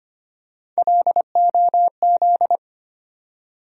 Morse code, heard as a keyed tone, about 25 wpm, 700 Hz. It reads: LOZ
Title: USA Call Sign Postfixes - 3 Letters 25wpm